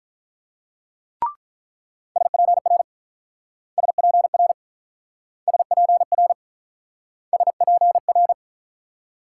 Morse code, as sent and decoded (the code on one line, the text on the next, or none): ... .--. .-. / ... .--. .-. / ... .--. .-. / ... .--. .-.
SPR SPR SPR SPR